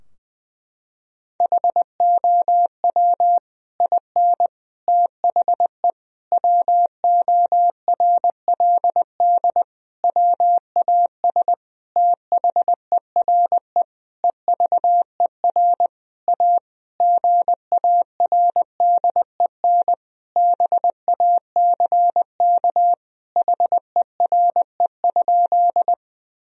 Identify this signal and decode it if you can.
Morse code, heard as a keyed tone, about 21 wpm, 700 Hz. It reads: HOW IN THE WORLD WAS THERE EVER A GARDEN BACK HERE?
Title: Sentences from Top 1000 Words - Review Part 4 - 20wpm